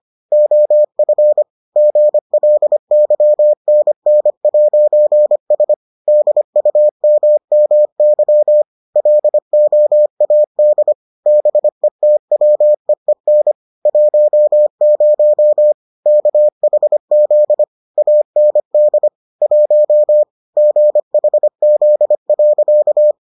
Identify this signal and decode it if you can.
Morse code, heard as a keyed tone, about 25 wpm, 600 Hz. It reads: OF GLYNN'S DUMMY LOAD BETWEEN 10 KHZ AND 1 GHZ.